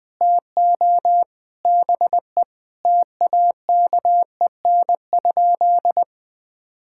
Morse code, heard as a keyed tone, about 20 wpm, 700 Hz. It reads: TO BE TAKEN?
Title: Sentences from Top 1000 Words - Review Part 6 - 20wpm